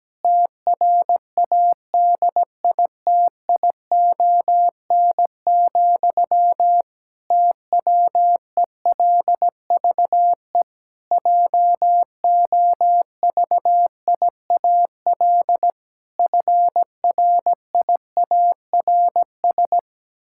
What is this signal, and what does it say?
Morse code, heard as a keyed tone, about 17 wpm, 700 Hz. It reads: TRADITION, TWELVE JOVIAL FRIARS
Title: Pangram Challenge v2 - 17wpm